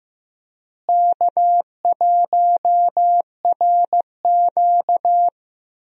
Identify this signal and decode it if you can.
Morse code, heard as a keyed tone, about 15 wpm, 700 Hz. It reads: K1RQ
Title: USA Call Signs - 4 Characters - Repeated 2x - 15wpm